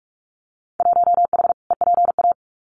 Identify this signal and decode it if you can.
Morse code, heard as a keyed tone, about 46 wpm, 700 Hz. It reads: 1H EPA
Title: Winter Field Day Exchanges 45wpm